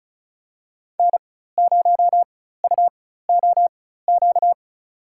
Morse code, heard as a keyed tone, about 36 wpm, spaced as for 16 wpm, 700 Hz. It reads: N0UOQ
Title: USA Call Signs - 5 Characters - 4x Character Spacing - 35wpm